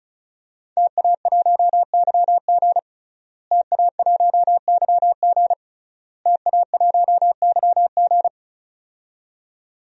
Morse code, heard as a keyed tone, about 36 wpm, 700 Hz. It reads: TA1YG TA1YG TA1YG
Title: DXCC-100 Call Signs - 5 Characters - Repeated 3x - 35wpm